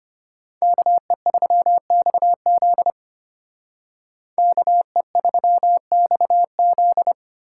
Morse code, heard as a keyed tone, about 33 wpm, 700 Hz. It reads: KE3XZ KE3XZ